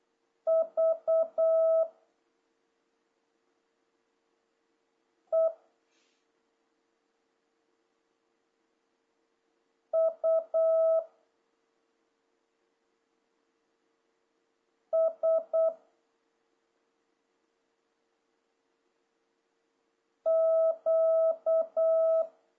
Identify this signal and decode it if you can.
Morse code, heard as a keyed tone, about 8 wpm, 650 Hz. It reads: V E U S Q
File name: jh8cbh / 2026cw